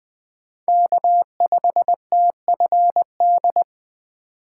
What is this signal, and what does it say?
Morse code, heard as a keyed tone, about 19 wpm, 700 Hz. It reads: K5TFD